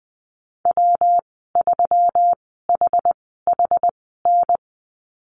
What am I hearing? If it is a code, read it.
Morse code, heard as a keyed tone, about 20 wpm, spaced as for 15 wpm, 700 Hz. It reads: W3HHN
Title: 147 Lesson 24g - Letter y - Call Signs 20wpm